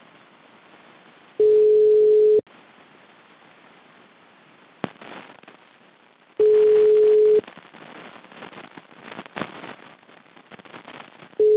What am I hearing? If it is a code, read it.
Ringback tone (Europe, 425 Hz).